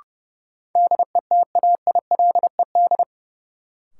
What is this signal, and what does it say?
Morse code, heard as a keyed tone, about 30 wpm, 700 Hz. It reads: DETAILED
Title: Sentences from Top 1500 Words - New Word Drill - 30wpm